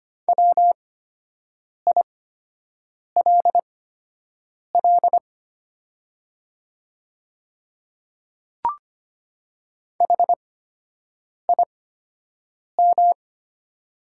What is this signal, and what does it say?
Morse code, heard as a keyed tone, about 25 wpm, spaced as for 7 wpm, 700 Hz. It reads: WILL HIM
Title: 8x Character Spacing - Top 100 Words - 25wpm